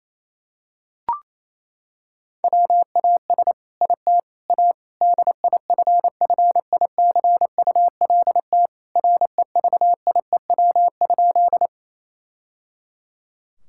Morse code, heard as a keyed tone, about 28 wpm, 700 Hz. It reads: WAS IT A DIFFICULT REVIEW?
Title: Sentences from Top 1000 Words - Review Part 6 - 28wpm